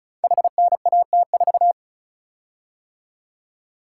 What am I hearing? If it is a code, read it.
Morse code, heard as a keyed tone, about 36 wpm, 700 Hz. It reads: HNAT4
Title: Code Groups - 5 Letter:Numbers - 35wpm